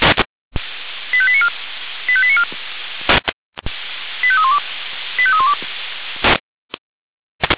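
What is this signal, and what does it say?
Five-tone selective call (ZVEI), heard as quick runs of five tones, tones of about 70 ms. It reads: 85894 85894 85312 85312
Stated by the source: ZVEI five-tone selective call 85310 received with an SDR, Feuerwehr Orlamünde fire brigade.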